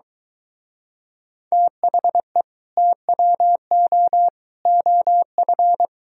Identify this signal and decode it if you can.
Morse code, heard as a keyed tone, about 23 wpm, 700 Hz. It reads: THE TWO OF